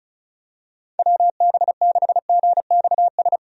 Morse code, heard as a keyed tone, about 34 wpm, 700 Hz. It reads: WB6GXS
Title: USA Call Signs - 6 Characters - 35wpm